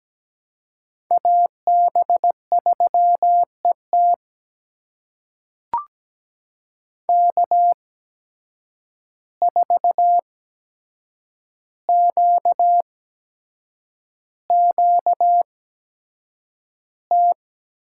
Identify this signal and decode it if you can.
Morse code, heard as a keyed tone, about 17 wpm, spaced as for 7 wpm, 700 Hz. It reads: AB3ET K4QQT
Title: USA Call Signs - 5 Characters - 8x Character Spacing - 17wpm